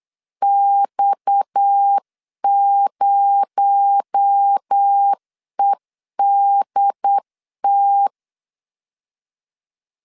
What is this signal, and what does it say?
Morse code, heard as a keyed tone, about 8 wpm, 800 Hz. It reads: X0EDT
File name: CW / DET / 1 6 1500C S C 8wpm 5219